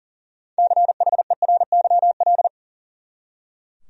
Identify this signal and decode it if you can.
Morse code, heard as a keyed tone, about 40 wpm, 700 Hz. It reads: CHERYL
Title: USA Common Names - 40wpm